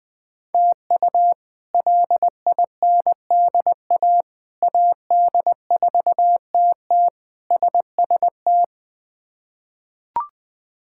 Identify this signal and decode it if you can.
Morse code, heard as a keyed tone, about 21 wpm, 700 Hz. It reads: TU LINDA AD4TT SST
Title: K1USN SST QSO Practice - 20wpm